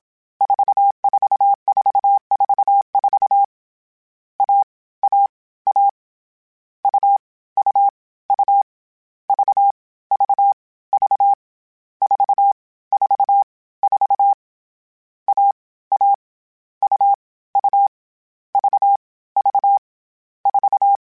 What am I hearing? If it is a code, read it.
Morse code, heard as a keyed tone, about 27 wpm, 800 Hz. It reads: 44444 A A A U U U V V V 4 4 4 A A U U V V 4
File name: Difficult Combos / Difficult Combos-mp3s / AUV AUV4 @25